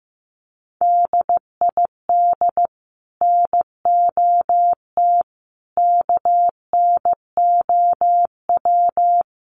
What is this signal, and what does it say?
Morse code, heard as a keyed tone, about 15 wpm, 700 Hz. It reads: DID NOT KNOW
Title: Sets of 4 Words 15wpm